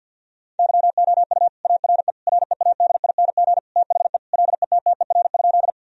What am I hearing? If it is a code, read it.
Morse code, heard as a keyed tone, about 50 wpm, 700 Hz. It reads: YOU ARE READING THE LETTER?
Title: Sentences from Top 1000 Words - Review Part 3 - 50wpm